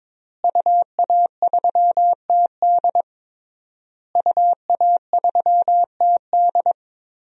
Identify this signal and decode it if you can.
Morse code, heard as a keyed tone, about 22 wpm, 700 Hz. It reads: UA3TD UA3TD
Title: DXCC-100 Call Signs - 5 Characters - Repeated 3x - 22wpm